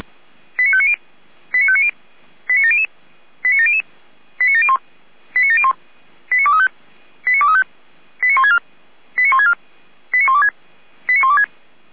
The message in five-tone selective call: 89590 89590 89700 89700 89701 89701 89235 89235 89164 89164 89126 89126

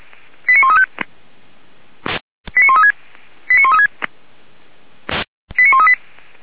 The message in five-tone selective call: 89135 89136 89136 89138

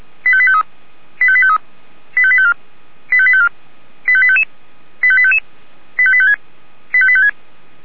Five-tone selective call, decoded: 85853 85853 85854 85854 85855 85855 85856 85856

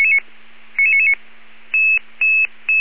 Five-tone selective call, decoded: 99999

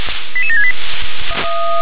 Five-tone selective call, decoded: 88768

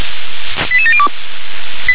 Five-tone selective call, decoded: 88702

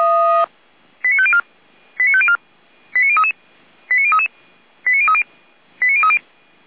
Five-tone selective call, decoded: 89504 89504 89033 89033 89030 89030

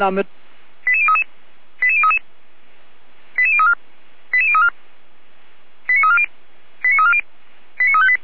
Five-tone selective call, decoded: 88033 88033 88034 88034 89340 89340 89358

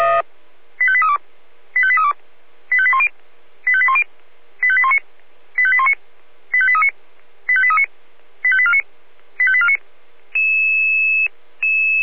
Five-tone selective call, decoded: 85832 85832 85810 85810 85819 85819 85839 85839 85849 85849

